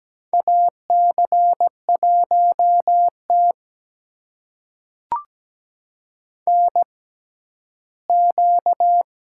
Morse code, heard as a keyed tone, about 17 wpm, spaced as for 11 wpm, 700 Hz. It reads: AC1T NQ
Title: USA Call Signs - 4 Characters - 6x Character Spacing - 17wpm